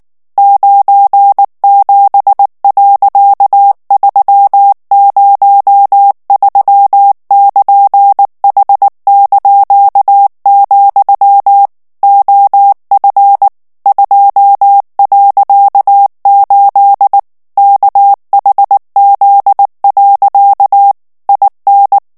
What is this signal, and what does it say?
Morse code, heard as a keyed tone, about 19 wpm, 800 Hz. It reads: 97.303<KN>H), OF 2.8 KHZ. IN
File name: ANGLAIS texte clair 3 vitesse 18 mots minute